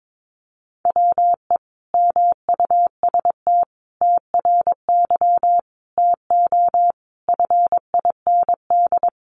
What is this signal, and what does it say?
Morse code, heard as a keyed tone, about 22 wpm, 700 Hz. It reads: WE MUST TRY TO FIND
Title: Sentences from Top 400 Words 22wpm